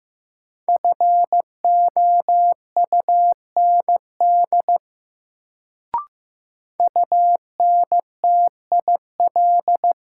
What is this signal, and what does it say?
Morse code, heard as a keyed tone, about 16 wpm, 700 Hz. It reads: FOUND UNTIL
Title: Sets of 5 Words Encore - New Word Drill - 15wpm